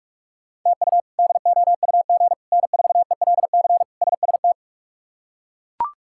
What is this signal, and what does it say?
Morse code, heard as a keyed tone, about 41 wpm, 700 Hz. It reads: TU DOUG N4ELC SST
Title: K1USN SST QSO Practice - 45wpm